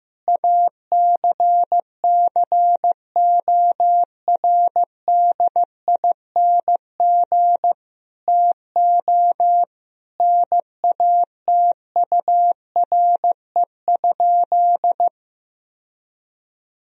Morse code, heard as a keyed tone, about 16 wpm, 700 Hz. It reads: ACCORDING TO NATURE?
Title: Sentences from Top 1000 Words - Review Part 4 - 15wpm